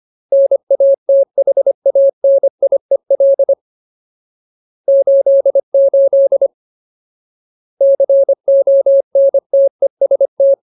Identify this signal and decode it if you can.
Morse code, heard as a keyed tone, about 25 wpm, 550 Hz. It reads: NATHANIEL 88 CONTEST
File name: Ultimate Word List / 25WPM Ultimate